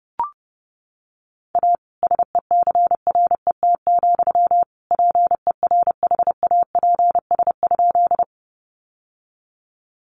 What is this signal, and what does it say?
Morse code, heard as a keyed tone, about 28 wpm, 700 Hz. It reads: A SECRET, PERHAPS?
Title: Sentences from Top 1200 Words 30wpm